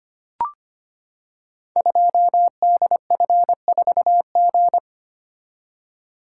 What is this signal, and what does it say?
Morse code, heard as a keyed tone, about 25 wpm, 700 Hz. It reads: 2DF4G
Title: Code Groups - 5 Letter:Numbers - 25wpm